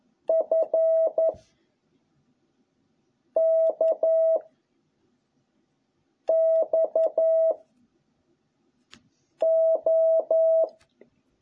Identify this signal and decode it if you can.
Morse code, heard as a keyed tone, about 11 wpm, 660 Hz. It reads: F K X O